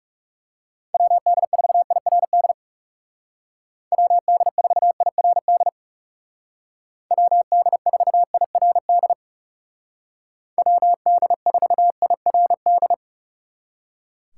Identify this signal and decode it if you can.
Morse code, heard as a keyed tone, about 38 wpm, 700 Hz. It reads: WD4IRD WD4IRD WD4IRD WD4IRD